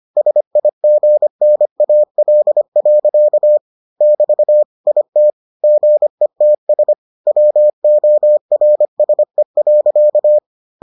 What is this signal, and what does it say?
Morse code, heard as a keyed tone, about 25 wpm, 600 Hz. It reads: SIGNAL. <BT> IT GETS WORSE.